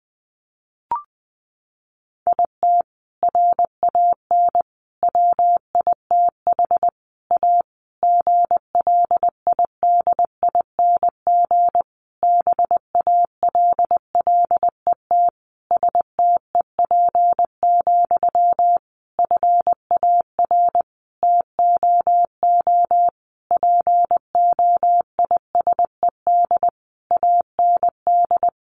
Morse code, heard as a keyed tone, about 19 wpm, 700 Hz. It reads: IT RAN WITH A GLIDING BALLET STEP, FAR TOO POISED AND
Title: A Sound Of Thunder 20wpm